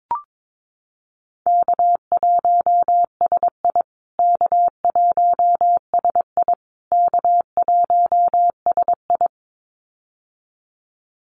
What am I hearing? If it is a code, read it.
Morse code, heard as a keyed tone, about 22 wpm, 700 Hz. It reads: K1SI K1SI K1SI
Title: USA Call Signs Repeated 3x 22wpm